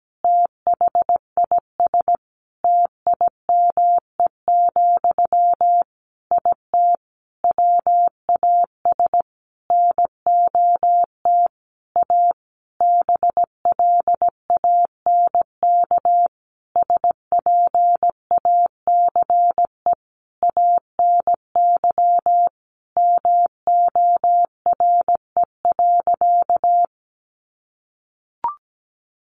Morse code, heard as a keyed tone, about 17 wpm, 700 Hz. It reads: THIS TIME, IT WAS NOT A BLANK SPACE ANY MORE.
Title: Sentences from Top 1500 Words 17wpm